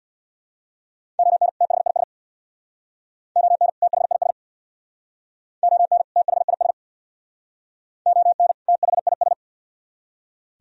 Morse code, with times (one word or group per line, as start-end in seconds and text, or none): 1.2-1.5 ON
1.6-2.0 THIS
3.4-3.7 ON
3.8-4.3 THIS
5.6-6.0 ON
6.2-6.7 THIS
8.1-8.5 ON
8.7-9.3 THIS